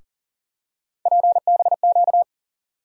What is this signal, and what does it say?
Morse code, heard as a keyed tone, about 40 wpm, 700 Hz. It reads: PBQ
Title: USA Call Sign Postfixes - 3 Letters 40wpm